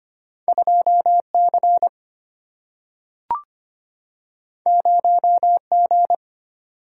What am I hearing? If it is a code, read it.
Morse code, heard as a keyed tone, about 25 wpm, 700 Hz. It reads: 2 C 0 G
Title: Single-Letter Single-Number 25wpm